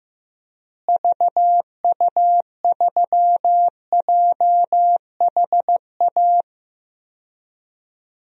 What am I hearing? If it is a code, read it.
Morse code, heard as a keyed tone, about 16 wpm, 700 Hz. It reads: VU3JHA
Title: DXCC-100 Call Signs - 6 Characters - Repeated 2x - 15wpm